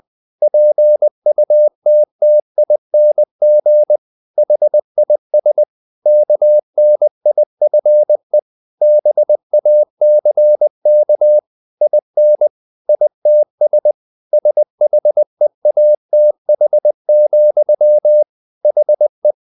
Morse code, heard as a keyed tone, about 20 wpm, 600 Hz. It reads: PUTTING HIS KNIFE BACK IN ITS SHEATH, HE
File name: the old man and the sea part 2 20wpm